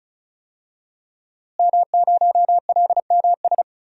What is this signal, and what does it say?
Morse code, heard as a keyed tone, about 34 wpm, 700 Hz. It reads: M0LMS
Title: DXCC-100 Call Signs - 5 Characters - Repeated 2x - 35wpm